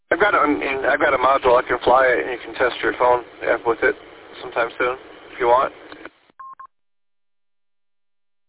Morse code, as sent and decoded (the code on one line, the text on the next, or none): none